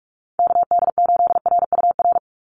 Morse code, heard as a keyed tone, about 44 wpm, 700 Hz. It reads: KD8RUR
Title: USA Call Signs 45wpm